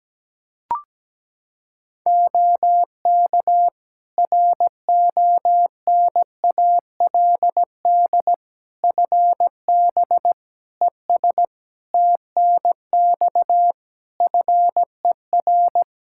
Morse code, heard as a keyed tone, about 17 wpm, 700 Hz. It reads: OK RONALD FB ES TNX FER
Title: Practice QSO - Exchange 2 - Info - 17wpm